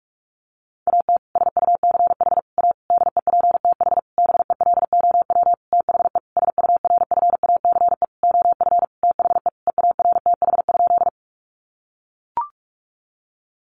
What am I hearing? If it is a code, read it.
Morse code, heard as a keyed tone, about 44 wpm, 700 Hz. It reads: AT SUCH A DEPTH BELOW THE SURFACE OF THE EARTH?
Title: Sentences from Top 1200 Words 45wpm